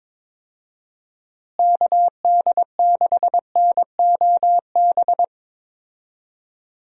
Morse code, heard as a keyed tone, about 22 wpm, 700 Hz. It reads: KD6NOB
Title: USA Call Signs - 6 Characters - Repeated 3x - 22wpm